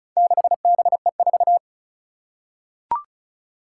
Morse code, heard as a keyed tone, about 36 wpm, 700 Hz. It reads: 6BE4
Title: Four Challenging Letter-Number Combinations 35wpm